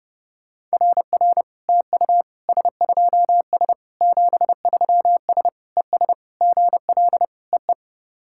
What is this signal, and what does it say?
Morse code, heard as a keyed tone, about 30 wpm, 700 Hz. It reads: RR TU S2S 73S ES GL EE
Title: SOTA QSO Practice 30wpm